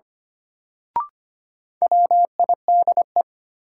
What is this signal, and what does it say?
Morse code, heard as a keyed tone, about 25 wpm, 700 Hz. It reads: WIDE
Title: Sentences from Top 900 Words - New Word Drill - 25wpm